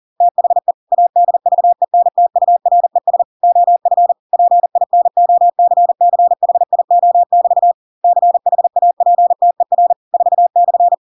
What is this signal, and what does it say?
Morse code, heard as a keyed tone, about 40 wpm, 700 Hz. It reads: THE ADVENTURES OF PINOCCHIO<BT> CHAPTER 4/
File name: Pinocchio / Pinocchio ch4 40 40 ik1hge